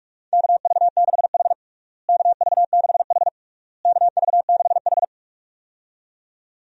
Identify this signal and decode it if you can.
Morse code, heard as a keyed tone, about 42 wpm, 700 Hz. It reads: KV6H KV6H KV6H
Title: USA Call Signs - 4 Characters - Repeated 3x - 45wpm